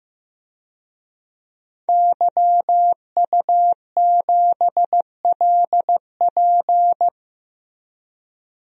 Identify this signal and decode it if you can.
Morse code, heard as a keyed tone, about 15 wpm, 700 Hz. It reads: YU7LP
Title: DXCC-100 Call Signs - 5 Characters - Repeated 2x - 15wpm